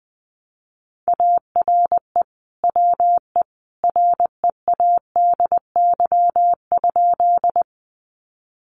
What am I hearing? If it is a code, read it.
Morse code, heard as a keyed tone, about 20 wpm, 700 Hz. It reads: ARE WE READY?
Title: Sentences from Top 400 Words 20wpm